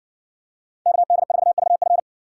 Morse code, heard as a keyed tone, about 60 wpm, 700 Hz. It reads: KD4VF